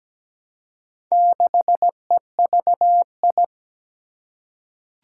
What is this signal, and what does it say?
Morse code, heard as a keyed tone, about 17 wpm, 700 Hz. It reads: 6EVI
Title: Four Challenging Letter-Number Combinations 17wpm